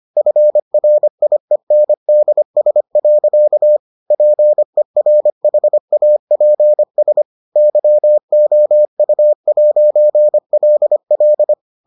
Morse code, heard as a keyed tone, about 25 wpm, 600 Hz. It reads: FRIENDS. PERHAPS YOU'LL